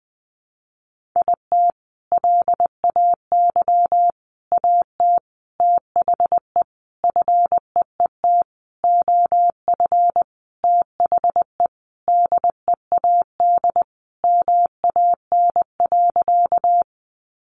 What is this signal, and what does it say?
Morse code, heard as a keyed tone, about 20 wpm, 700 Hz. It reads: IT LAY AT THE FEET OF THE DEAD MAN.
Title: Sentences from Top 1500 Words 20wpm